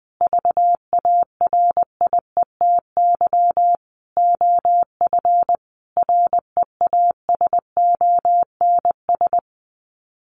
Morse code, heard as a keyed tone, about 19 wpm, 700 Hz. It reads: VARIETY OF REASONS
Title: Sets of 5 Words Encore 20wpm